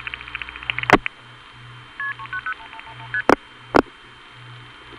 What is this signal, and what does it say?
Multi-frequency trunk signalling, heard as fast pairs of tones, tones of about 80 ms. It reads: KP390 ST